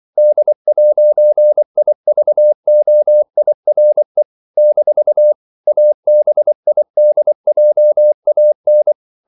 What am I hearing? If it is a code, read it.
Morse code, heard as a keyed tone, about 24 wpm, 600 Hz. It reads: D'IVOIRE - ABIDJAN
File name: CW / Practice / NationalCapitals24